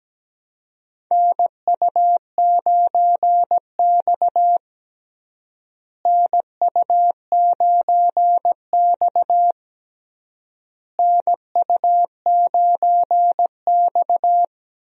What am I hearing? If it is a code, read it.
Morse code, heard as a keyed tone, about 17 wpm, 700 Hz. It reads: NU9X NU9X NU9X
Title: USA Call Signs - 4 Characters - Repeated 3x - 17wpm